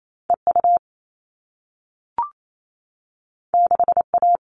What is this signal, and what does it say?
Morse code, heard as a keyed tone, about 28 wpm, 700 Hz. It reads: EU 6A
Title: All Cut Numbers - 2-Digit 28wpm